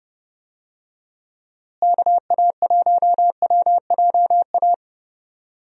Morse code, heard as a keyed tone, about 30 wpm, 700 Hz. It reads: KA1WJA
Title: USA Call Signs - 6 Characters - Repeated 3x - 30wpm